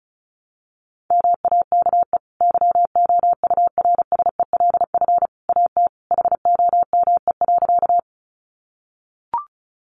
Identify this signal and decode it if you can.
Morse code, heard as a keyed tone, about 34 wpm, 700 Hz. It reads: MAKE YOURSELF AT HOME.
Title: Sentences from Top 600 Words 35wpm